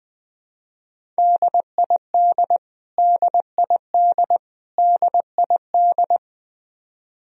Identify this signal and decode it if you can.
Morse code, heard as a keyed tone, about 20 wpm, 700 Hz. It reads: DID DID DID
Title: Top 100 Words Repeated 20wpm